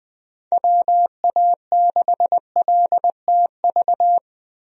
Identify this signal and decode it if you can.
Morse code, heard as a keyed tone, about 20 wpm, 700 Hz. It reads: WA6LTV
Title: USA Call Signs - 6 Characters - Repeated 3x - 20wpm